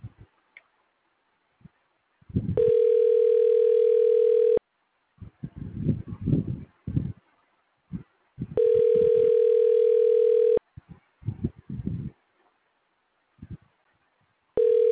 Ringback tone (North America), 440 + 480 Hz.